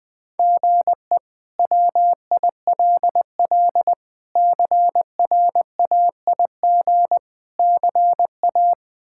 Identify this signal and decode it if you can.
Morse code, heard as a keyed tone, about 20 wpm, 700 Hz. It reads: GE WILL CRAIG CA
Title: K1USN SST QSO Practice - 20wpm